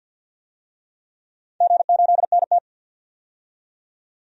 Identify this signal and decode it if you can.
Morse code, heard as a keyed tone, about 53 wpm, 700 Hz. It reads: G8NT
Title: DXCC-100 Call Signs - 4 Characters - 50wpm